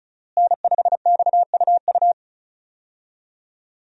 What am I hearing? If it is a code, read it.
Morse code, heard as a keyed tone, about 35 wpm, 700 Hz. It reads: N5XUU
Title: USA Call Signs - 5 Characters - Repeated 2x - 35wpm